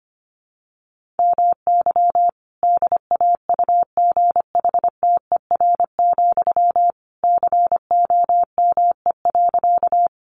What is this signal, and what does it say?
Morse code, heard as a keyed tone, about 25 wpm, 700 Hz. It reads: MY DAUGHTER, COME.
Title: Sentences from Top 1500 Words 25wpm